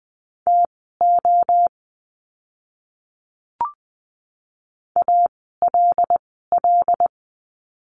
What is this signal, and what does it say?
Morse code, heard as a keyed tone, about 20 wpm, spaced as for 15 wpm, 700 Hz. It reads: TO ALL
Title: 146 Lesson 24f - Letter y - Words 20wpm